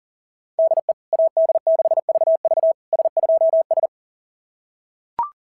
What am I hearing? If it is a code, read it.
Morse code, heard as a keyed tone, about 40 wpm, 690 Hz. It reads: DE AD6VV S2S
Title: SOTA QSO Practice 40wpm